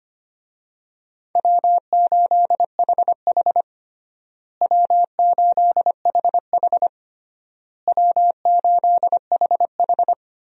Morse code, heard as a keyed tone, about 25 wpm, 700 Hz. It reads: W8HH W8HH W8HH
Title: USA Call Signs - 4 Characters - Repeated 3x - 25wpm